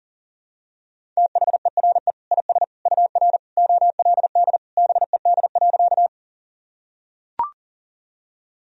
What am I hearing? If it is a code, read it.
Morse code, heard as a keyed tone, about 40 wpm, 700 Hz. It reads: THERE IS UR OLD BED.
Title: Sentences from Top 500 Words and Abbreviations 40wpm